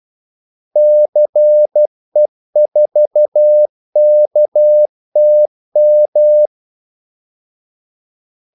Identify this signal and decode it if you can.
Morse code, heard as a keyed tone, about 12 wpm, 600 Hz. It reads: CE4KTM
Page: 2 ekstra ordmellemrum mellem kaldesignalerne